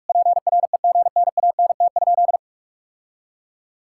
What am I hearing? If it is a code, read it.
Morse code, heard as a keyed tone, about 46 wpm, 690 Hz. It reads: PREGNANT?
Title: Sentences from Top 1000 Words - Review Part 6 - 45wpm